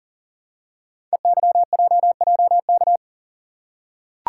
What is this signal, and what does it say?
Morse code, heard as a keyed tone, about 41 wpm, 690 Hz. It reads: EYJJK
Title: Code Groups - 5 Letters - 40wpm